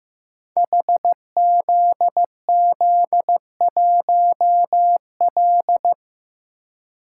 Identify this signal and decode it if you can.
Morse code, heard as a keyed tone, about 15 wpm, 700 Hz. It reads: HZZ1L
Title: Code Groups - 5 Letter:Numbers - 15wpm